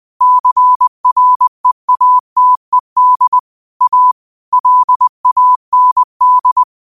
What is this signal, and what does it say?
Morse code, heard as a keyed tone, about 20 wpm, 1000 Hz. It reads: CREATED A LAND
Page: Quotes for Fri, 15 Aug 2025 in Morse Code at 20 words per minute.